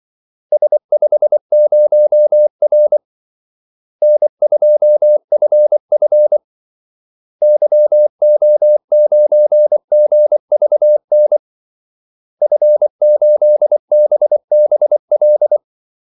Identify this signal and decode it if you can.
Morse code, heard as a keyed tone, about 24 wpm, 600 Hz. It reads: S50R N2FF YO9GVN F8BBL
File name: Callsigns-24wpm